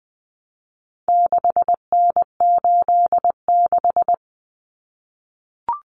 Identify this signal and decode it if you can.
Morse code, heard as a keyed tone, about 20 wpm, 700 Hz. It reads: 6N86
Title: Common Cut Numbers - 4-Digit 20wpm